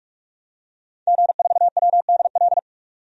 Morse code, heard as a keyed tone, about 43 wpm, 700 Hz. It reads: G4WDL
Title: DXCC-100 Call Signs - 5 Characters - 45wpm